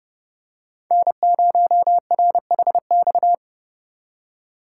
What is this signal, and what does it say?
Morse code, heard as a keyed tone, about 32 wpm, 700 Hz. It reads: N0RHX
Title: USA Call Signs - 5 Characters - 3x Character Spacing - 30wpm